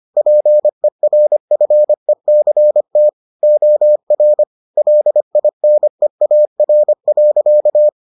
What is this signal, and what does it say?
Morse code, heard as a keyed tone, about 25 wpm, 600 Hz. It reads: PERFECT OR LINEAR.